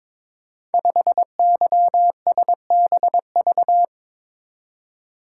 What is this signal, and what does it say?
Morse code, heard as a keyed tone, about 23 wpm, 700 Hz. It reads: 5YSBV
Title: Code Groups - 5 Letter:Numbers - 22wpm